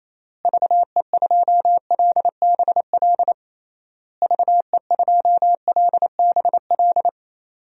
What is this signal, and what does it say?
Morse code, heard as a keyed tone, about 27 wpm, 700 Hz. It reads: VE2LBL VE2LBL
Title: DXCC-100 Call Signs - 6 Characters - Repeated 3x - 28wpm